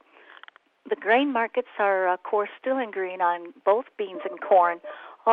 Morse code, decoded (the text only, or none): none